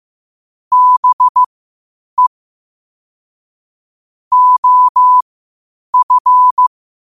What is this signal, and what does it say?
Morse code, heard as a keyed tone, about 15 wpm, spaced as for 8 wpm, 1000 Hz. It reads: BE OF
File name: qotdfiles / 8wpm